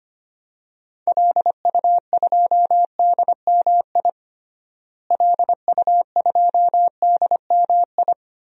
Morse code, heard as a keyed tone, about 25 wpm, 700 Hz. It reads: LU2DMI LU2DMI
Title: DXCC-100 Call Signs - 6 Characters - Repeated 3x - 25wpm